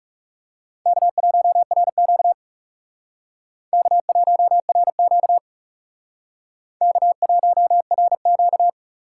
Morse code, heard as a keyed tone, about 36 wpm, 700 Hz. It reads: K1RQ K1RQ K1RQ
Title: Speed Racing - USA Call Signs 30wpm